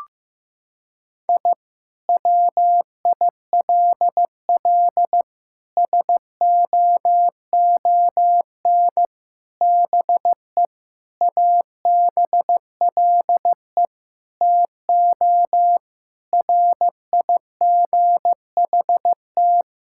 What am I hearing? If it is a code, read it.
Morse code, heard as a keyed tone, about 15 wpm, 700 Hz. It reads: I WILL SOON BE ABLE TO RIGHT
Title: Sentences from Top 1000 Words - Review Part 6 - 15wpm